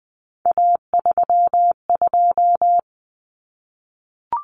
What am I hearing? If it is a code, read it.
Morse code, heard as a keyed tone, about 20 wpm, 700 Hz. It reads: A32
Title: Common Cut Numbers - 3-Digit 20wpm